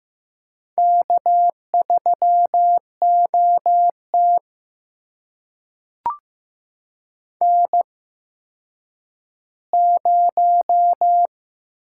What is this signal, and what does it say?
Morse code, heard as a keyed tone, about 15 wpm, spaced as for 9 wpm, 700 Hz. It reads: K3OT N0
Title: USA Call Signs - 8x Character Spacing - 15wpm